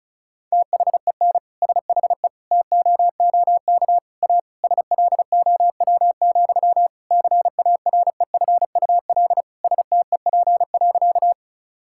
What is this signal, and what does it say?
Morse code, heard as a keyed tone, about 34 wpm, 700 Hz. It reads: THEN SHE TOOK A SLOW, CAREFUL STEP.
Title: Sentences from Top 1600 Words 35wpm